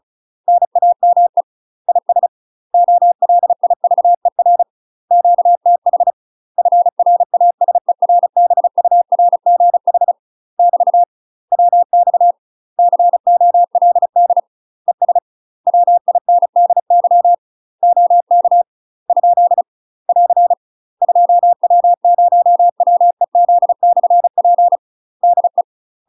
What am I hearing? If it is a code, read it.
Morse code, heard as a keyed tone, about 36 wpm, 700 Hz. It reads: NAME IS OLIVER QTH FRASERBURGH <BT> WX COLD ES WINDY OK ? <AR> 2W0WEZ/P DE
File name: UK MASTER QSOs TRAINING/35WPM